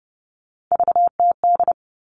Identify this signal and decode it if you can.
Morse code, heard as a keyed tone, about 30 wpm, 700 Hz. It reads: VTD
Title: All Cut Numbers - 3-Digit 30wpm